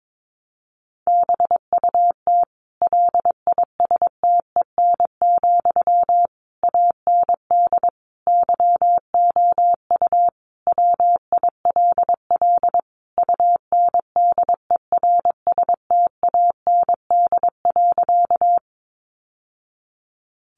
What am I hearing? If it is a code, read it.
Morse code, heard as a keyed tone, about 22 wpm, 700 Hz. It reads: BUT LISTEN, AND YOU WILL UNDERSTAND.
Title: Sentences from Top 600 Words 22wpm